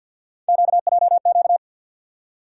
Morse code, heard as a keyed tone, about 47 wpm, 700 Hz. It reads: QJQ
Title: USA Call Sign Postfixes - 3 Letters 50wpm